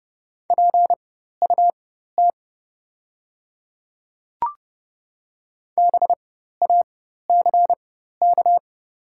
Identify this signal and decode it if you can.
Morse code, heard as a keyed tone, about 30 wpm, spaced as for 14 wpm, 700 Hz. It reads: PUT BACK
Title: 4x Character Spacing - Top 500 Words - 30wpm